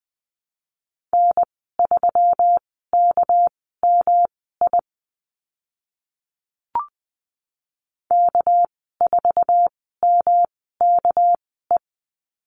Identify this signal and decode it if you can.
Morse code, heard as a keyed tone, about 20 wpm, spaced as for 14 wpm, 700 Hz. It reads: N3KMI K4MKE
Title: 200 Lesson 32g - Letter k - Call Signs 20wpm